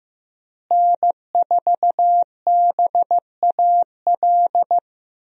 Morse code, heard as a keyed tone, about 15 wpm, 700 Hz. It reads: N4BAL
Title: USA Call Signs - 5 Characters - Repeated 3x - 15wpm